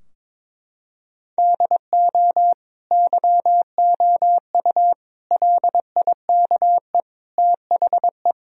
Morse code, heard as a keyed tone, about 22 wpm, 700 Hz. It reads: DO YOU LIKE THE
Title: Sentences from Top 1000 Words - Review Part 4 - 22wpm